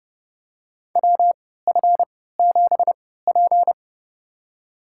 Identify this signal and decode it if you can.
Morse code, heard as a keyed tone, about 30 wpm, spaced as for 17 wpm, 700 Hz. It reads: WF7P
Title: USA Call Signs - 4 Characters - 3x Character Spacing - 30wpm